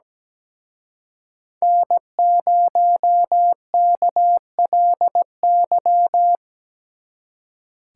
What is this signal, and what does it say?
Morse code, heard as a keyed tone, about 20 wpm, 700 Hz. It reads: N0KLY
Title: Speed Racing - USA 1x3 Call Signs 17wpm